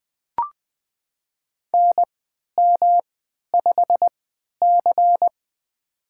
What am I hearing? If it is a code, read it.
Morse code, heard as a keyed tone, about 20 wpm, spaced as for 11 wpm, 700 Hz. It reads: NM5C
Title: USA Call Signs - 4 Characters - 3x Character Spacing - 20wpm